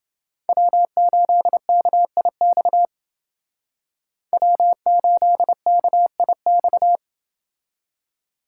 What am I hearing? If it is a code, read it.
Morse code, heard as a keyed tone, about 30 wpm, 700 Hz. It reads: W8KIX W8KIX